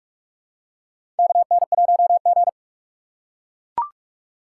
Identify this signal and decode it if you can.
Morse code, heard as a keyed tone, about 44 wpm, 690 Hz. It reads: KN1G
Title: USA Call Signs - 4 Characters - 2x Character Spacing - 45wpm